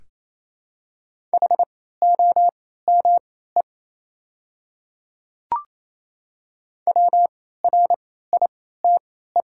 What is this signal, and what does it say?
Morse code, heard as a keyed tone, about 28 wpm, spaced as for 16 wpm, 700 Hz. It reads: HOME WRITE
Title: 3x Character Spacing - Top 200 Words - 28wpm